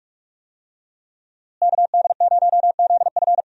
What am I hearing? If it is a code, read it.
Morse code, heard as a keyed tone, about 41 wpm, 700 Hz. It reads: KD0ZF
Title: USA Call Signs - 5 Characters - Repeated 3x - 45wpm